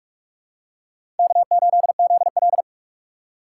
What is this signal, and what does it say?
Morse code, heard as a keyed tone, about 42 wpm, 700 Hz. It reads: K8ZL
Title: USA Call Signs - 4 Characters - 4x Character Spacing - 45wpm